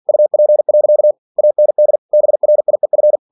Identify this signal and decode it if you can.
Morse code, heard as a keyed tone, about 50 wpm, 600 Hz. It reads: UP. AND BRIEF
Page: Never 47wpm